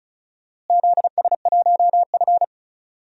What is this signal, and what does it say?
Morse code, heard as a keyed tone, about 36 wpm, 700 Hz. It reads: ZS1F
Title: DXCC-100 Call Signs - 4 Characters - Repeated 3x - 35wpm